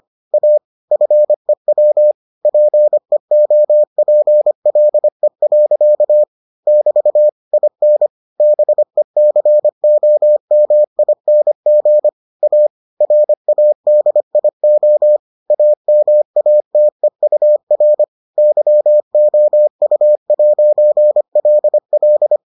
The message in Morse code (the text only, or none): A FEW PEOPLE. <BT> IN BECOMING A RADIO AMATEUR YOU'LL